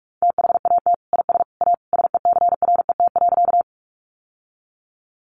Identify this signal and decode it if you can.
Morse code, heard as a keyed tone, about 44 wpm, 700 Hz. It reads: THAT IS A SECRET.
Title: Sentences from Top 1200 Words 45wpm